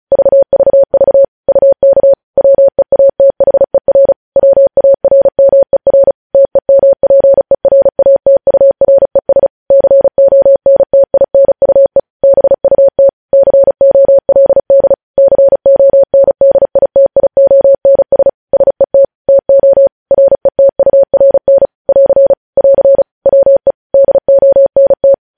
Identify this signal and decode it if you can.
Morse code, heard as a keyed tone, about 36 wpm, 550 Hz. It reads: VVV UK WEATHER WARMER TEMPERATURES CONTINUE BUT COLD CONDITIONS SET TO RETURN <AR> <AR> WE DONT
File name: podcast / News headlines-1x-35x35